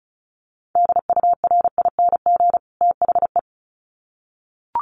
Morse code, heard as a keyed tone, about 34 wpm, 700 Hz. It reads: DURING THE
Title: Sets of 2 Words 35wpm